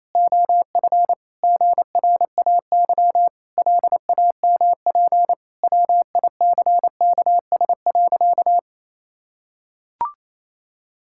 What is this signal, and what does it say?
Morse code, heard as a keyed tone, about 28 wpm, 700 Hz. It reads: OF GRAY LAMP WICKS.
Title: Pangram Challenge v2 - 28wpm